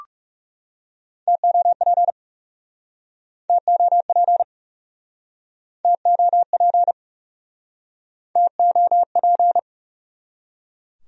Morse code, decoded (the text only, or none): TOP TOP TOP TOP